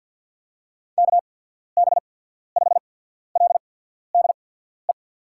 Morse code, heard as a keyed tone, about 53 wpm, spaced as for 14 wpm, 700 Hz. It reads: KB5LDE
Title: USA Call Signs - 6 Characters - 8x Character Spacing - 50wpm